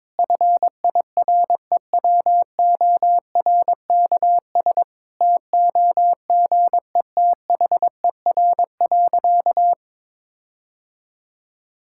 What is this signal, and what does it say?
Morse code, heard as a keyed tone, about 22 wpm, 700 Hz. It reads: FIREWORKS TOGETHER.
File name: Difficult Sentences with Hints - 22wpm